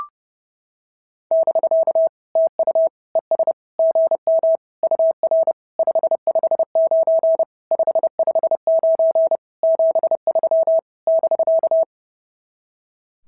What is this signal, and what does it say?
Morse code, heard as a keyed tone, about 28 wpm, 650 Hz. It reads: <BK> TU ES GM UR 559 559 73 <BK>
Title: SOTA QSO Practice 30wpm